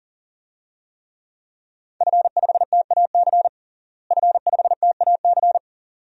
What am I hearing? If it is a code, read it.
Morse code, heard as a keyed tone, about 40 wpm, 700 Hz. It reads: F5TAC F5TAC
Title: DXCC-100 Call Signs - 5 Characters - Repeated 2x - 40wpm